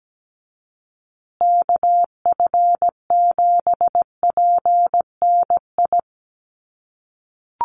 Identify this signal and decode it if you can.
Morse code, heard as a keyed tone, about 17 wpm, 700 Hz. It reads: KF7PNI
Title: USA Call Signs Repeated 3x 17wpm